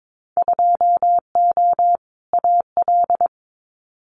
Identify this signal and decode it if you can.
Morse code, heard as a keyed tone, about 22 wpm, 700 Hz. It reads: 2O AL
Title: Winter Field Day Exchanges 22wpm